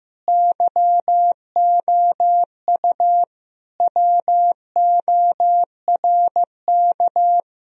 Morse code, heard as a keyed tone, about 15 wpm, 700 Hz. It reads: YOU WORK
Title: Sentences from Top 1000 Words - Review Part 1 - 15wpm